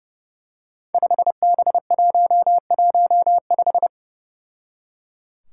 Morse code, heard as a keyed tone, about 30 wpm, 700 Hz. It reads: 5B115
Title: Five Challenging Letter-Number Combinations 30wpm